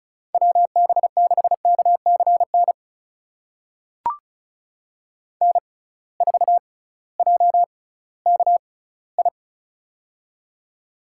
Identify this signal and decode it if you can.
Morse code, heard as a keyed tone, about 36 wpm, spaced as for 19 wpm, 700 Hz. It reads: WB6KCN N4JKI
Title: USA Call Signs - 6x Character Spacing - 35wpm